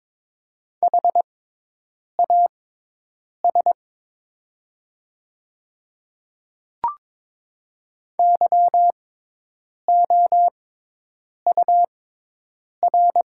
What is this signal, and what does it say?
Morse code, heard as a keyed tone, about 22 wpm, spaced as for 8 wpm, 700 Hz. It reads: HAS YOUR
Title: 6x Character Spacing - Top 100 Words - 22wpm